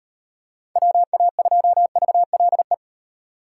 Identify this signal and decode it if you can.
Morse code, heard as a keyed tone, about 35 wpm, 700 Hz. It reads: WA2VLE